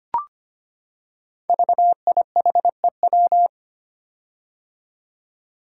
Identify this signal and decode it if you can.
Morse code, heard as a keyed tone, about 25 wpm, 700 Hz. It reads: VIHEW
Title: Code Groups - 5 Letter:Numbers - 25wpm